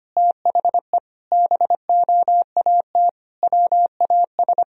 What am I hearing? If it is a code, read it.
Morse code, heard as a keyed tone, about 25 wpm, 700 Hz. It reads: THE BOAT WAS
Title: Sentences from Top 1000 Words - Review Part 3 - 25wpm